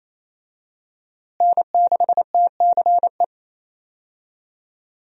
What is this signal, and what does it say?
Morse code, heard as a keyed tone, about 28 wpm, 700 Hz. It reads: N6TCE
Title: USA Call Signs - 5 Characters - Repeated 3x - 28wpm